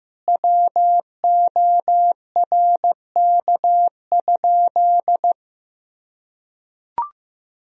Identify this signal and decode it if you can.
Morse code, heard as a keyed tone, about 16 wpm, 700 Hz. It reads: WORK?
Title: Sentences from Top 1000 Words - Review Part 4 - 15wpm